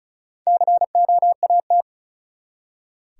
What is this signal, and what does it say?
Morse code, heard as a keyed tone, about 36 wpm, 700 Hz. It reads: COAT
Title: Four Letter Words - 35wpm